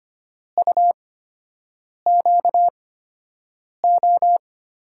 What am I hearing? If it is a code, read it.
Morse code, heard as a keyed tone, about 25 wpm, spaced as for 7 wpm, 700 Hz. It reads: UQO
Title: USA Call Signs - 5 Characters - 8x Character Spacing - 25wpm